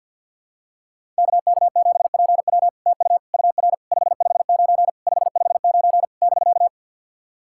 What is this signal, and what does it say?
Morse code, heard as a keyed tone, about 48 wpm, 700 Hz. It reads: KK7PW TU UR 559 559 <BK>
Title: SOTA QSO Practice 50wpm